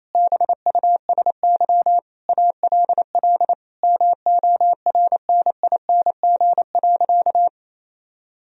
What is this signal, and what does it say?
Morse code, heard as a keyed tone, about 28 wpm, 700 Hz. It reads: BUSY ALL MORNING.
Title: Sentences from Top 1000 Words - Review Part 3 - 28wpm